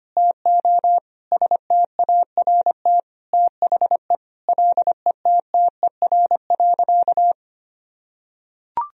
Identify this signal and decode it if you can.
Morse code, heard as a keyed tone, about 25 wpm, 700 Hz. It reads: TO START THE LETTER.
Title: Sentences from Top 1000 Words - Review Part 1 - 25wpm